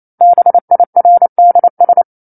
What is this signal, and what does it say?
Morse code, heard as a keyed tone, about 28 wpm, 700 Hz. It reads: BIRDS
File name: 09-Chapter-09 28 WPM